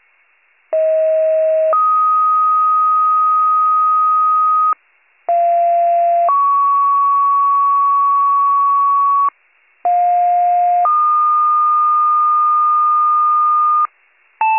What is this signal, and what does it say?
Two-tone page, heard as a one-second tone, then a three-second tone, about 640 then 1250 Hz, about 690 then 1120 Hz, about 710 then 1220 Hz. It is heard from 0.7 to 13.9 s.